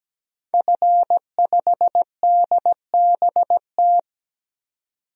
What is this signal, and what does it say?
Morse code, heard as a keyed tone, about 17 wpm, 700 Hz. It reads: F5DBT
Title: DXCC-100 Call Signs - 5 Characters - 17wpm